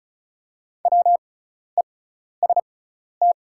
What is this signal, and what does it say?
Morse code, heard as a keyed tone, about 36 wpm, spaced as for 12 wpm, 690 Hz. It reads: WEST
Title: 6x Character Spacing - Top 500 Words - 35wpm